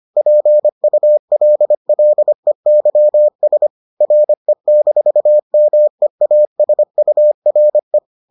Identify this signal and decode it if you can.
Morse code, heard as a keyed tone, about 25 wpm, 600 Hz. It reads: PULLEYS RE-MEASURE